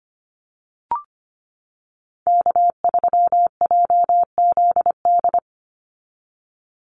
Morse code, heard as a keyed tone, about 25 wpm, 700 Hz. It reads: K3JZD
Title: NA SOTA Chaser Callsigns 25wpm